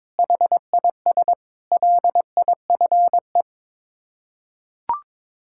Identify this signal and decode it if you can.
Morse code, heard as a keyed tone, about 22 wpm, 700 Hz. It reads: HIS LIFE
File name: practice / 22wpm / Sets of 6 Words - 22wpm